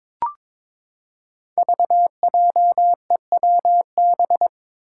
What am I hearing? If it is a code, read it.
Morse code, heard as a keyed tone, about 21 wpm, 700 Hz. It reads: VJEWB
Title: Five Challenging Letter-Number Combinations 22wpm